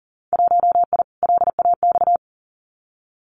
Morse code, heard as a keyed tone, about 41 wpm, 700 Hz. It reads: 1I LAX
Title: Winter Field Day Exchanges 40wpm